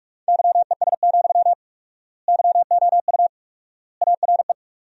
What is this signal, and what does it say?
Morse code, heard as a keyed tone, about 42 wpm, 690 Hz. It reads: YES, YOU ARE
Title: 4x Word-Spacing - Sentences from Top 500 Words - 45wpm